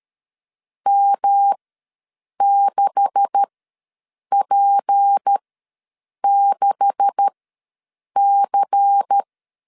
Morse code, heard as a keyed tone, about 13 wpm, spaced as for 7 wpm, 800 Hz. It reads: M6P6C
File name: LEZIONI / 9 286 ws 12wpm 3351